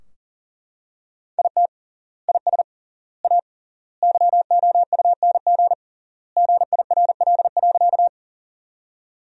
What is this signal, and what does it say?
Morse code, heard as a keyed tone, about 40 wpm, 700 Hz. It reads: IT IS A YOUNG GIRL.
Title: 3x Word-Spacing - Sentences from Top 400 Words - 40wpm